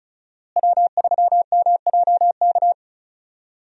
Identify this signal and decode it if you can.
Morse code, heard as a keyed tone, about 36 wpm, 700 Hz. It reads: W3MJK
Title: USA Call Signs - 5 Characters - Repeated 2x - 35wpm